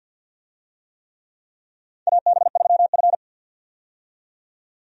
Morse code, heard as a keyed tone, about 51 wpm, 700 Hz. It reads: AB3F
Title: USA Call Signs - 4 Characters - 50wpm